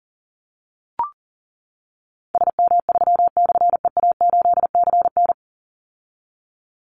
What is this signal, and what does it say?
Morse code, heard as a keyed tone, about 40 wpm, 700 Hz. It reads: SM3/EA8CN
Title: Call Signs - 40wpm